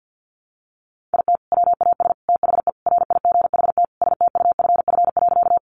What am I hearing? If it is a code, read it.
Morse code, heard as a keyed tone, about 50 wpm, 700 Hz. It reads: IT WAS THE RIGHT STUFF.
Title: Sentences from Top 1400 Words 50wpm